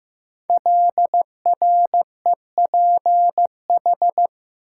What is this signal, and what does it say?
Morse code, heard as a keyed tone, about 15 wpm, 700 Hz. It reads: LREPH
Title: Code Groups - 5 Letters - 15wpm